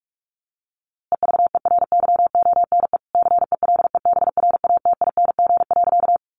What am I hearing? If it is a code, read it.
Morse code, heard as a keyed tone, about 44 wpm, 700 Hz. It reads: EVERYONE CELEBRATING.
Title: A Sound Of Thunder 45wpm